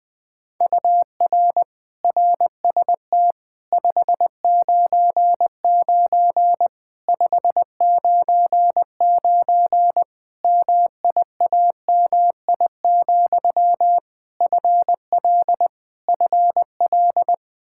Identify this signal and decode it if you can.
Morse code, heard as a keyed tone, about 21 wpm, 700 Hz. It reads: UR RST 599 599 MIAMI, FL FL
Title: Practice QSO - Exchange 1 - Beginning - 20wpm